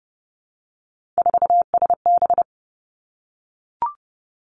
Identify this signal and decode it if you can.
Morse code, heard as a keyed tone, about 31 wpm, 700 Hz. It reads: 4SB
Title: Three Challenging Letter-Number Combinations 30wpm